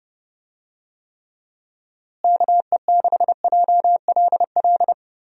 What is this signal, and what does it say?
Morse code, heard as a keyed tone, about 30 wpm, 700 Hz. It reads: KE6JLL
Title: USA Call Signs - 6 Characters - Repeated 3x - 30wpm